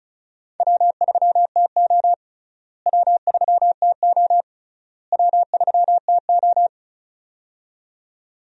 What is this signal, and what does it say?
Morse code, heard as a keyed tone, about 34 wpm, 700 Hz. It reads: W3TO W3TO W3TO
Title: USA Call Signs - 4 Characters - Repeated 3x - 35wpm